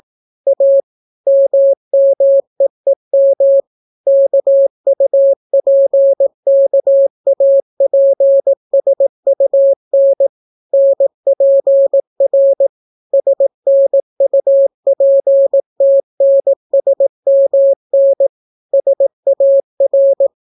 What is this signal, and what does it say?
Morse code, heard as a keyed tone, about 18 wpm, 550 Hz. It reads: A MMEEM KUPKAPSUN NPR SNUPTNSMN SAR